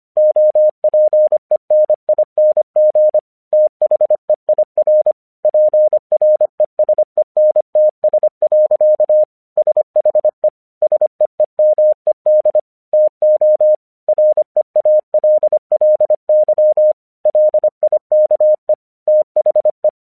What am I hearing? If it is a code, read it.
Morse code, heard as a keyed tone, about 24 wpm, 610 Hz. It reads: OPENING THEIR PRESENTS. SHE SEEMED TO REALLY LIKE THE
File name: 1 English 25wpm / AQC20